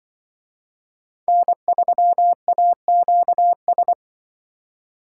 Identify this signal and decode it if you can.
Morse code, heard as a keyed tone, about 24 wpm, 700 Hz. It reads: N3AQS
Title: Speed Racing - USA 1x3 Call Signs 20wpm